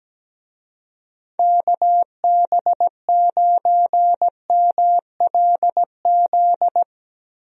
Morse code, heard as a keyed tone, about 17 wpm, 700 Hz. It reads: KB9MLZ
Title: USA Call Signs - 6 Characters - 17wpm